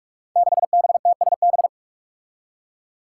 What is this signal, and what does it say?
Morse code, heard as a keyed tone, about 44 wpm, 700 Hz. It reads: 6BTSB
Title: Five Challenging Letter-Number Combinations 45wpm